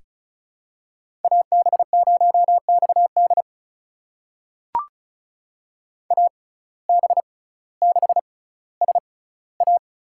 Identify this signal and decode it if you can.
Morse code, heard as a keyed tone, about 36 wpm, spaced as for 18 wpm, 700 Hz. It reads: AB0XD AB6SA
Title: USA Call Signs - 5 Characters - 6x Character Spacing - 35wpm